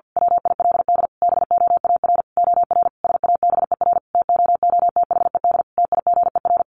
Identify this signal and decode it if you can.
Morse code, heard as a keyed tone, about 50 wpm, 700 Hz. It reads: WILD BOAR OR SABER TOOTHED TIGER
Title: A Sound Of Thunder 50wpm